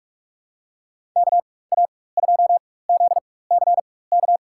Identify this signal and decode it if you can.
Morse code, heard as a keyed tone, about 45 wpm, spaced as for 21 wpm, 700 Hz. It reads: KA2ZCK
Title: USA Call Signs - 4x Character Spacing - 45wpm